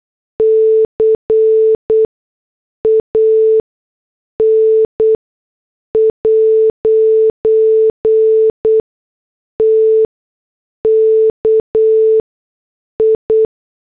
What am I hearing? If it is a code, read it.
Morse code, heard as a keyed tone, about 8 wpm, 440 Hz. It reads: C A N ' T K I
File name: morse / headlines